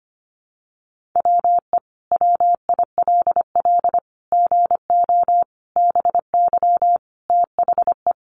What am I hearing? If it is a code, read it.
Morse code, heard as a keyed tone, about 25 wpm, 700 Hz. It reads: WE WILL GO BY THE
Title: Sentences from Top 1500 Words 25wpm